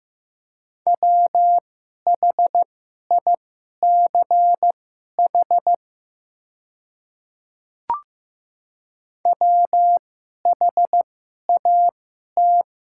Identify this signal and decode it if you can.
Morse code, heard as a keyed tone, about 15 wpm, spaced as for 11 wpm, 690 Hz. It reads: WHICH WHAT
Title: 2x Character Spacing - Top 100 Words - 15wpm